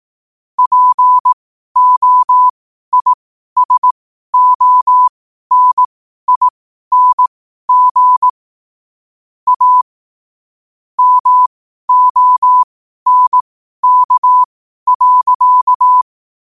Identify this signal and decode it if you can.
Morse code, heard as a keyed tone, about 18 wpm, spaced as for 12 wpm, 1000 Hz. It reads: POISONING A MONK.
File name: qotdfiles / 12wpm